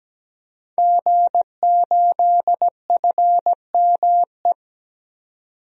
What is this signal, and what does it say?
Morse code, heard as a keyed tone, about 17 wpm, 700 Hz. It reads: G8FME
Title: DXCC-100 Call Signs - 5 Characters - 17wpm